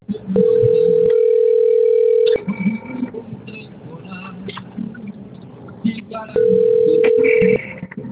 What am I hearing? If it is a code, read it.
Ringback tone (North America), 440 + 480 Hz.